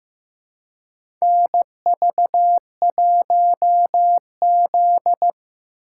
Morse code, heard as a keyed tone, about 15 wpm, 700 Hz. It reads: NV1Z
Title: USA Call Signs - 4 Characters - 3x Character Spacing - 15wpm